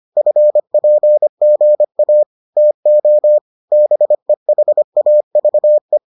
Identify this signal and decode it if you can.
Morse code, heard as a keyed tone, about 25 wpm, 600 Hz. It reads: FPGA TO BEHAVE